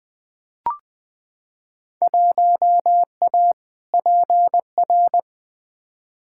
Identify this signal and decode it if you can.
Morse code, heard as a keyed tone, about 20 wpm, 700 Hz. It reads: 1A PR
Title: Field Day Exchanges Encore - 20wpm